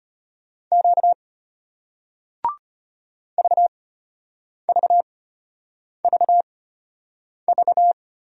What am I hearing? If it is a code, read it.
Morse code, heard as a keyed tone, about 33 wpm, spaced as for 8 wpm, 700 Hz. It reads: Q VVVV